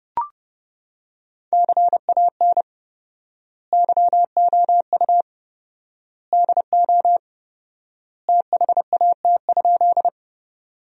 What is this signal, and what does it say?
Morse code, heard as a keyed tone, about 32 wpm, 700 Hz. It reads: CAN YOU DO THAT?
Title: 4x Word-Spacing - Sentences from Top 100 Words - 30wpm